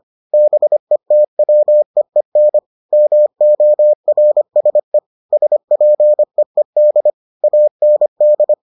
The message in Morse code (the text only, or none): BETWEEN MORSE SPEED AND